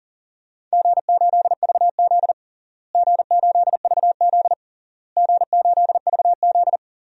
Morse code, heard as a keyed tone, about 37 wpm, 700 Hz. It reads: G8VZ G8VZ G8VZ
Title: DXCC-100 Call Signs - 4 Characters - Repeated 3x - 40wpm